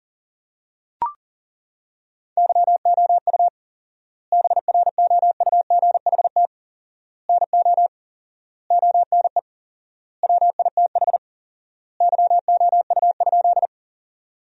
Morse code, heard as a keyed tone, about 37 wpm, 700 Hz. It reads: YOU BROUGHT NO ONE WITH YOU?
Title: 4x Word-Spacing - Sentences from Top 500 Words - 40wpm